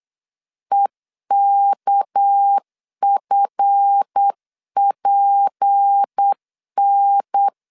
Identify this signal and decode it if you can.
Morse code, heard as a keyed tone, about 8 wpm, 800 Hz. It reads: EKFPN
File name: CW / DET / 1 9 1500C S A 8wpm 5637